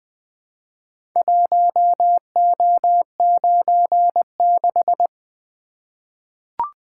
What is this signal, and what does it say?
Morse code, heard as a keyed tone, about 20 wpm, 700 Hz. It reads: 1O96
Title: Four Letter-Number Combinations - 20wpm